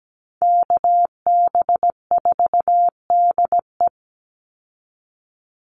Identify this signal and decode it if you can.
Morse code, heard as a keyed tone, about 17 wpm, 700 Hz. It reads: KB4DE
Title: Call Signs - 17wpm